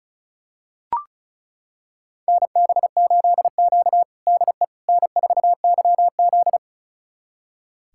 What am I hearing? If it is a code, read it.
Morse code, heard as a keyed tone, about 36 wpm, 700 Hz. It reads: NB8Q DE N4YZ
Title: USA Callsign DE Callsign - 4 Characters - 35wpm